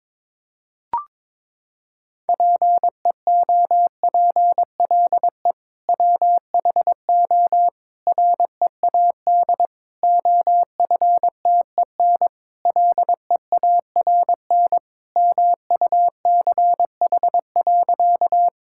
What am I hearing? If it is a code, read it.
Morse code, heard as a keyed tone, about 21 wpm, 700 Hz. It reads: PEOPLE WHO READ OFTEN LEARN MUCH.
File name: practice / 22wpm / Sentences with 2-Word Phrases - 22wpm